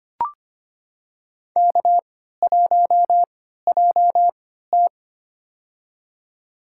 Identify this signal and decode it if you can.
Morse code, heard as a keyed tone, about 25 wpm, spaced as for 14 wpm, 700 Hz. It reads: K1JT
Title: USA Call Signs - 3x Character Spacing - 25wpm